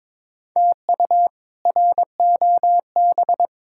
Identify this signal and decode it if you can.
Morse code, heard as a keyed tone, about 22 wpm, 700 Hz. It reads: TU ROB
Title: K1USN SST QSO Practice - 22wpm